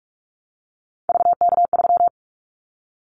Morse code, heard as a keyed tone, about 46 wpm, 700 Hz. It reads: VK3
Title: Callsign Prefixes for Top 100 DXCC - 45wpm